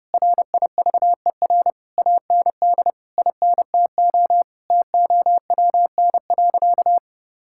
Morse code, heard as a keyed tone, about 30 wpm, 700 Hz. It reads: RIVER AND INTO TOWN.
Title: Sentences from Top 1000 Words - Review Part 6 - 30wpm